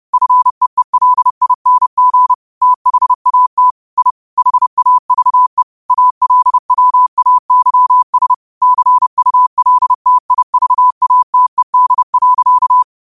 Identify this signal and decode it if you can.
Morse code, heard as a keyed tone, about 30 wpm, 1000 Hz. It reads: FEELING THAT I HAVE ALWAYS CULTIVATED.
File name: qotdfiles / 30wpm